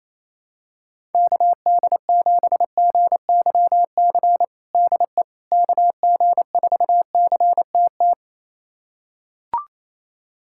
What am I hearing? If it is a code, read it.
Morse code, heard as a keyed tone, about 28 wpm, 700 Hz. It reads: KD7GYC DE KG4CTT
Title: USA Callsign DE Callsign - 6 Characters - 28wpm